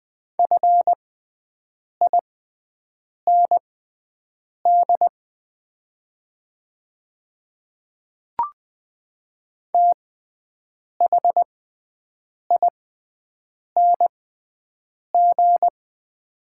Morse code, heard as a keyed tone, about 20 wpm, spaced as for 7 wpm, 700 Hz. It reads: FIND THING
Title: 6x Character Spacing - Top 100 Words - 20wpm